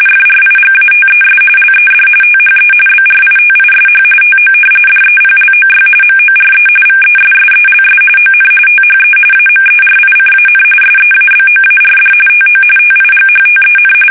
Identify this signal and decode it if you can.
RTTY, about 45.45 baud. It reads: FRIGATES LED THE BRITISH TO DEPLOY MORE VESSELS ON THE AMERICAN SEABOARD TO TIGHTEN
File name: fsk850 / usnavy2